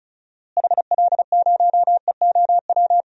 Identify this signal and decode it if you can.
Morse code, heard as a keyed tone, about 36 wpm, 700 Hz. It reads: HL0EOW
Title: DXCC-100 Call Signs - 6 Characters - Repeated 2x - 35wpm